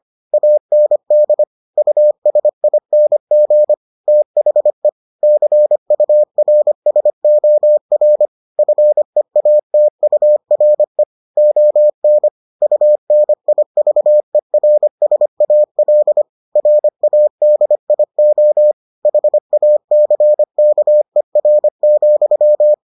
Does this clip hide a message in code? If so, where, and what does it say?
Yes: Morse code from 0.3 to 22.7 s, heard as a keyed tone, about 25 wpm, 600 Hz. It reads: AND USING THE CURSOR FEATURE ON UNIVERSAL RADIO HACKER,